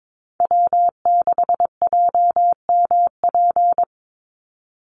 Morse code, heard as a keyed tone, about 22 wpm, 700 Hz. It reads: W6JMP
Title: NA SOTA Chaser Callsigns 22wpm